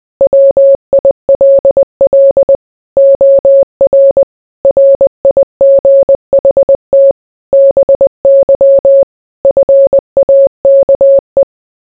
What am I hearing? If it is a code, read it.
Morse code, heard as a keyed tone, about 20 wpm, 550 Hz. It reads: WILL OR RIGHT BY FAKE